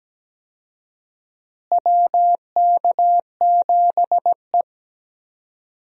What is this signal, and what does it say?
Morse code, heard as a keyed tone, about 17 wpm, 700 Hz. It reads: WK7E
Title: USA Call Signs - 4 Characters - Repeated 3x - 17wpm